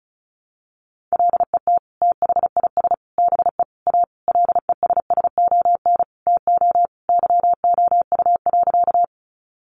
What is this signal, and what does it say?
Morse code, heard as a keyed tone, about 34 wpm, 700 Hz. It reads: LET THIS BE A LESSON TO YOU.
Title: Sentences from Top 1200 Words 35wpm